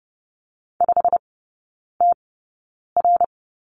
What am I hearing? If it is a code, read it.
Morse code, heard as a keyed tone, about 30 wpm, spaced as for 9 wpm, 700 Hz. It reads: 5TR
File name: Lesson 19e - Letter w - Three Characters 30wpm